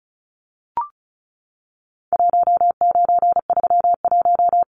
Morse code, heard as a keyed tone, about 34 wpm, 700 Hz. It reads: 1931
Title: Numbers - 4-Digit 35wpm